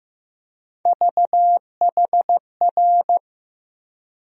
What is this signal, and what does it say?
Morse code, heard as a keyed tone, about 15 wpm, 700 Hz. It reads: VHR
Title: USA Call Sign Postfixes - 3 Letters 15wpm